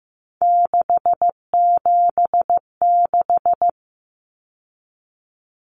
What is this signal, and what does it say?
Morse code, heard as a keyed tone, about 15 wpm, 700 Hz. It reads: 676
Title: Numbers - 3-Digit 15wpm